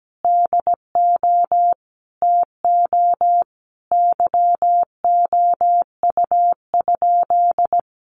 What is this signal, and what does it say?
Morse code, heard as a keyed tone, about 17 wpm, 700 Hz. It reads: DO TO YOU?
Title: Sentences from Top 100 Words 17wpm